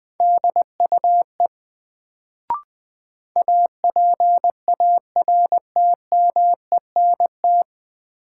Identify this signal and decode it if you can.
Morse code, heard as a keyed tone, about 20 wpm, 700 Hz. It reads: DUE APARTMENT
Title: Sentences from Top 900 Words - New Word Drill - 20wpm